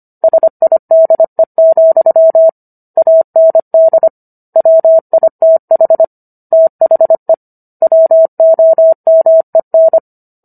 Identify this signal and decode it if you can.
Morse code, heard as a keyed tone, about 25 wpm, 650 Hz. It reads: SIDE, AND WITH THE WOMEN